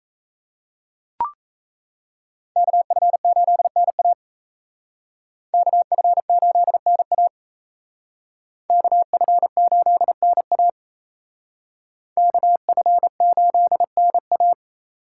Morse code, heard as a keyed tone, about 35 wpm, 700 Hz. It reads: KF8NA KF8NA KF8NA KF8NA